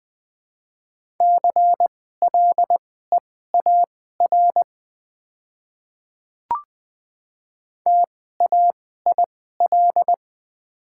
Morse code, heard as a keyed tone, about 20 wpm, spaced as for 14 wpm, 690 Hz. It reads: CLEAR TAIL
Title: 2x Character Spacing - Top 500 Words - 20wpm